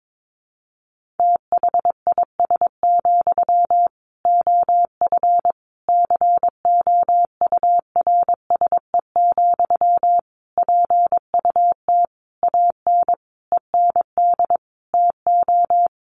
Morse code, heard as a keyed tone, about 22 wpm, 700 Hz. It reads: THIS, OF COURSE, PUT AN END TO
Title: Sentences from Top 1400 Words 22wpm